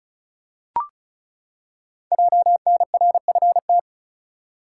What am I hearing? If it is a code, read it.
Morse code, heard as a keyed tone, about 36 wpm, 700 Hz. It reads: JNRFT
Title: Code Groups - 5 Letter:Numbers - 35wpm